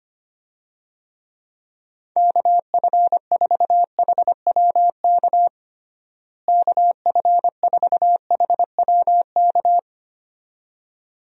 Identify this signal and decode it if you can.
Morse code, heard as a keyed tone, about 25 wpm, 700 Hz. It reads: KF4HWK KF4HWK
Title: USA Call Signs - 6 Characters - Repeated 2x - 25wpm